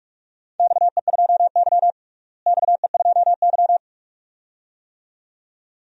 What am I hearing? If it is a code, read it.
Morse code, heard as a keyed tone, about 46 wpm, 700 Hz. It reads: XE2Y XE2Y
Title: DXCC-100 Call Signs - 4 Characters - Repeated 2x - 45wpm